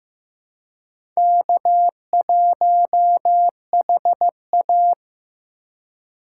Morse code, heard as a keyed tone, about 15 wpm, 700 Hz. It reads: K1HA
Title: USA Call Signs - 4 Characters - 3x Character Spacing - 15wpm